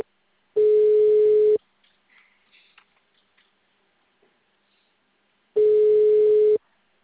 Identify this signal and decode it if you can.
Ringback tone (Europe, 425 Hz).
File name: out- -2030- - - .89